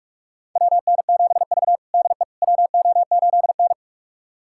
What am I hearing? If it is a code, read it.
Morse code, heard as a keyed tone, about 41 wpm, 700 Hz. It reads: WN7V DE WO8N
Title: USA Callsign DE Callsign - 4 Characters - 45wpm